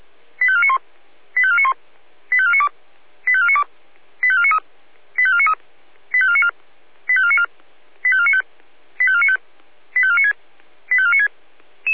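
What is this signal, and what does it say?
Five-tone selective call (ZVEI), heard as quick runs of five tones, tones of about 70 ms. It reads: 85491 85491 85492 85492 85493 85493 85494 85494 85495 85495 85496 85496
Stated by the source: Five-tone ZVEI selective call 85363 received on SDR.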